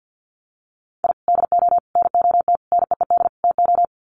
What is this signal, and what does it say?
Morse code, heard as a keyed tone, about 50 wpm, 700 Hz. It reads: I DO NOT NEED TO
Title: Sets of 5 Words 50wpm